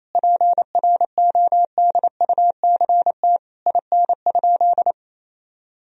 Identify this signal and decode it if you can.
Morse code, heard as a keyed tone, about 28 wpm, 700 Hz. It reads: PRODUCT IN?
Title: Sentences from Top 1000 Words - Review Part 2 - 28wpm